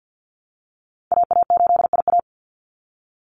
Morse code, heard as a keyed tone, about 49 wpm, 700 Hz. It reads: AA8IA
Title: Call Signs Repeated 3x - 50wpm